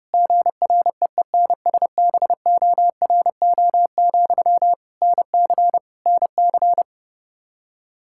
Morse code, heard as a keyed tone, about 30 wpm, 700 Hz. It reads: GREENSBORO, NC NC
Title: K1USN SST QSO Practice 30wpm